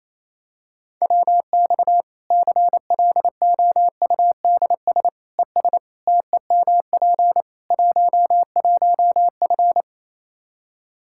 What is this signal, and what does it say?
Morse code, heard as a keyed tone, about 27 wpm, 700 Hz. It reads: WX CLOUDS ES TEMP 11F
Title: Practice QSO - Exchange 2 - Info - 28wpm